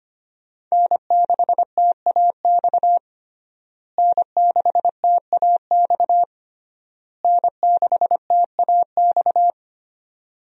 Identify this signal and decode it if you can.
Morse code, heard as a keyed tone, about 25 wpm, 700 Hz. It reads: N6TAX N6TAX N6TAX
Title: USA Call Signs - 5 Characters - Repeated 3x - 25wpm